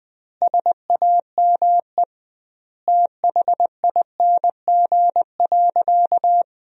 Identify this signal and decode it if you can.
Morse code, heard as a keyed tone, about 20 wpm, 700 Hz. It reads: SAME THING.
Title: 2x Word-Spacing - Sentences from Top 200 Words - 20wpm